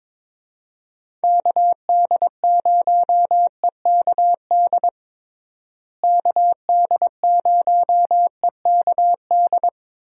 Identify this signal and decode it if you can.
Morse code, heard as a keyed tone, about 22 wpm, 700 Hz. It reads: KD0EKD KD0EKD
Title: USA Call Signs - 6 Characters - Repeated 3x - 22wpm